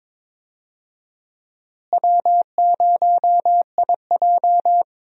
Morse code, heard as a keyed tone, about 22 wpm, 700 Hz. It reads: W0IJ
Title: USA Call Signs - 4 Characters - Repeated 2x - 22wpm